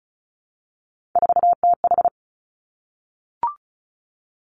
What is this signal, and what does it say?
Morse code, heard as a keyed tone, about 36 wpm, 700 Hz. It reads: 4TH
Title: Three Challenging Letter-Number Combinations 35wpm